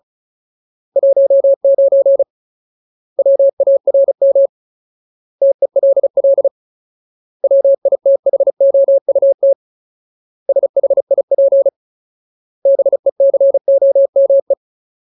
Morse code, heard as a keyed tone, about 34 wpm, 550 Hz. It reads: 19 WARM TELL WITHOUT SHIP BECOME
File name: Ultimate Word List / 35WPM Ultimate